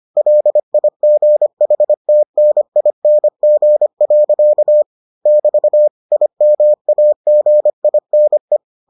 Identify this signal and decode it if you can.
Morse code, heard as a keyed tone, about 25 wpm, 600 Hz. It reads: LIGHTNING. <BT> IMAGINE